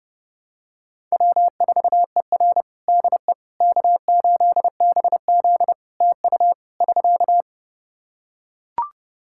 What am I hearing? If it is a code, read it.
Morse code, heard as a keyed tone, about 32 wpm, 700 Hz. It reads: W4ER DE K8BZ TU <SK>
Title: Practice QSO - Exchange 4 - Ending - 30wpm